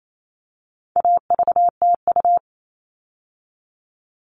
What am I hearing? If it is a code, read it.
Morse code, heard as a keyed tone, about 28 wpm, 700 Hz. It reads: AVTU
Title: All Cut Numbers - 4-Digit 28wpm